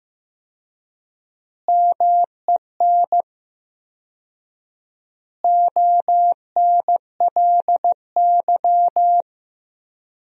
Morse code, heard as a keyed tone, about 15 wpm, 700 Hz. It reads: MEN ONLY
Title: 4x Word-Spacing - Sentences from Top 500 Words - 15wpm